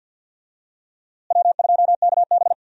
Morse code, heard as a keyed tone, about 50 wpm, 700 Hz. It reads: W2KB
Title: USA Call Signs - 4 Characters - Repeated 2x - 50wpm